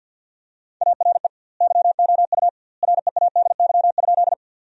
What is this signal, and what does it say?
Morse code, heard as a keyed tone, about 50 wpm, 700 Hz. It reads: ARE YOU READY?
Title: 2x Word-Spacing - Sentences from Top 400 Words - 50wpm